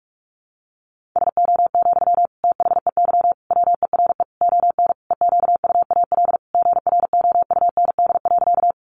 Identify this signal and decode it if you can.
Morse code, heard as a keyed tone, about 44 wpm, 700 Hz. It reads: SO, THEY WERE ON EQUAL GROUND.
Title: Sentences from Top 1400 Words 45wpm